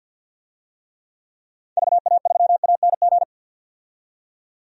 Morse code, heard as a keyed tone, about 46 wpm, 700 Hz. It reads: VA3ANG
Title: DXCC-100 Call Signs - 6 Characters - 50wpm